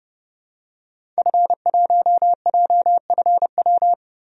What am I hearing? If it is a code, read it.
Morse code, heard as a keyed tone, about 31 wpm, 700 Hz. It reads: F1JFW
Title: DXCC-100 Call Signs - 5 Characters - 30wpm